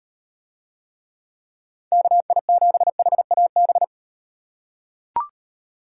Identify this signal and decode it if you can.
Morse code, heard as a keyed tone, about 35 wpm, 700 Hz. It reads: KI7HAB